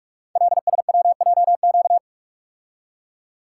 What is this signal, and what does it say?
Morse code, heard as a keyed tone, about 44 wpm, 700 Hz. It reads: LSWJQ
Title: Code Groups - 5 Letter:Numbers - 45wpm